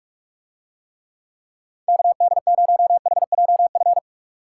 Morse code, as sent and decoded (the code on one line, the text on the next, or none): -.- -.. ----- .... .--- ..-.
KD0HJF